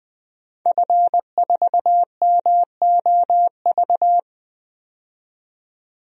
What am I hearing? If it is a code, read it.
Morse code, heard as a keyed tone, about 20 wpm, 700 Hz. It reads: F4MOV
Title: DXCC-100 Call Signs - 5 Characters - 20wpm